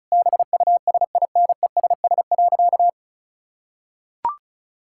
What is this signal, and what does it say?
Morse code, heard as a keyed tone, about 36 wpm, 700 Hz. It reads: BUSINESS.
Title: Sentences from Top 1600 Words 35wpm